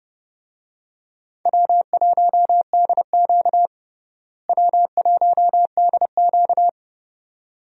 Morse code, heard as a keyed tone, about 31 wpm, 700 Hz. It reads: W1DQ W1DQ
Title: USA Call Signs - 4 Characters - Repeated 2x - 30wpm